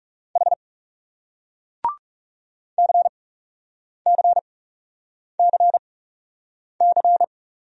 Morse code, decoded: H CCCC

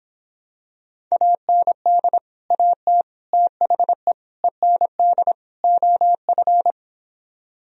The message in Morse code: AND AT THE END OF

